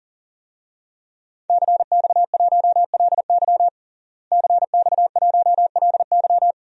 Morse code, heard as a keyed tone, about 40 wpm, 700 Hz. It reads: CX1LY CX1LY
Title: DXCC-100 Call Signs - 5 Characters - Repeated 2x - 40wpm